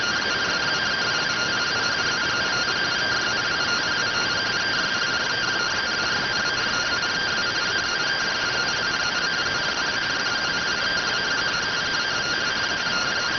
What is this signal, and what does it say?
Two-tone teleprinter signal, about 170 Hz shift.